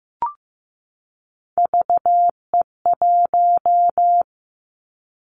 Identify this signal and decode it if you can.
Morse code, heard as a keyed tone, about 15 wpm, 700 Hz. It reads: VE1
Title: Callsign Prefixes for Top 25 DXCC - 15wpm